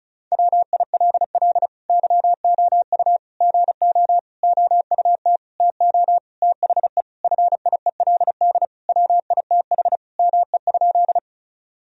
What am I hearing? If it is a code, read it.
Morse code, heard as a keyed tone, about 36 wpm, 700 Hz. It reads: WILL YOU GO OUT TO THE FIELD WITH ME?
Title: Sentences from Top 1000 Words - Review Part 2 - 35wpm